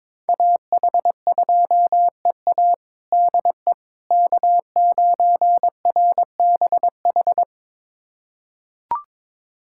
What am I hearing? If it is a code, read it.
Morse code, heard as a keyed tone, about 21 wpm, 700 Hz. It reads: AH2EA DE K9RBH
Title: USA Callsign DE Callsign - 5 Characters - 22wpm